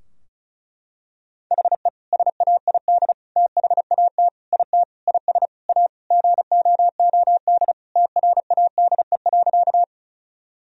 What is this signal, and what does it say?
Morse code, heard as a keyed tone, about 36 wpm, 700 Hz. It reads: HE SAID THAT IT IS A GOOD TRADE.
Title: Sentences from Top 1000 Words - Review Part 4 - 35wpm